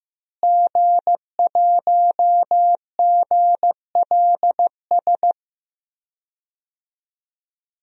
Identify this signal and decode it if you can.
Morse code, heard as a keyed tone, about 15 wpm, 700 Hz. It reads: G1GLS
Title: DXCC-100 Call Signs - 5 Characters - Repeated 2x - 15wpm